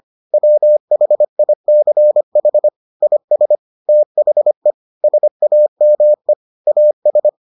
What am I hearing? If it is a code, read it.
Morse code, heard as a keyed tone, about 25 wpm, 600 Hz. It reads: WHICH IS THE SAME AS